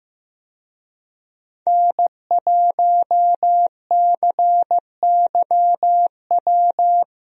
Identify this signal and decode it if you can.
Morse code, heard as a keyed tone, about 15 wpm, 700 Hz. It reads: N1CYW
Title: USA Call Signs - 5 Characters - 15wpm